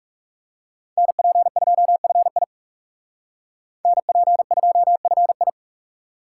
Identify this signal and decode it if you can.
Morse code, heard as a keyed tone, about 36 wpm, 700 Hz. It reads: NP2FI NP2FI
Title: Speed Racing - USA 2x2 Call Signs 30wpm